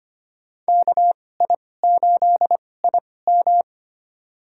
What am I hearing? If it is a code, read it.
Morse code, heard as a keyed tone, about 25 wpm, spaced as for 18 wpm, 700 Hz. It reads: KI8IM
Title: USA Call Signs - 5 Characters - 2x Character Spacing - 25wpm